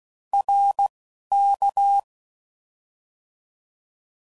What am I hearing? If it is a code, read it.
Morse code, heard as a keyed tone, about 16 wpm, 800 Hz. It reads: R K
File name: Mixed / Short Runs / ShortContrasts